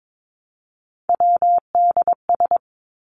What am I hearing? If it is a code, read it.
Morse code, heard as a keyed tone, about 22 wpm, 700 Hz. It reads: WDS
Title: Abbreviations Repeated 3x - 22wpm